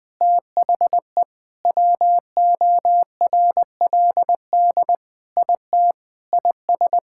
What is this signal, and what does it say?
Morse code, heard as a keyed tone, about 21 wpm, 700 Hz. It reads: THE WORLD IT IS
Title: Sentences with 6-Word Phrases - Spoken Beginning - 20wpm